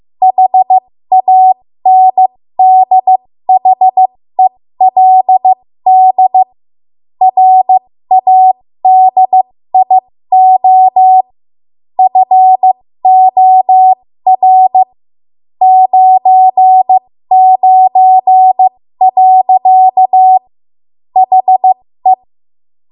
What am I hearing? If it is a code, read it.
Morse code, heard as a keyed tone, about 15 wpm, spaced as for 13 wpm, 750 Hz. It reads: HANDHELD RADIO FOR 99. HE
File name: Morse / 13WPM